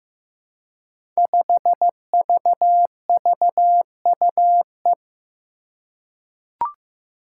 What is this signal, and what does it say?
Morse code, heard as a keyed tone, about 15 wpm, 700 Hz. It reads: 5VVUE
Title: Five Challenging Letter-Number Combinations 15wpm